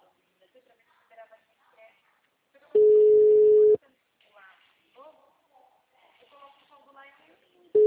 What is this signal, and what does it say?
Ringback tone (Europe, 425 Hz), with speech.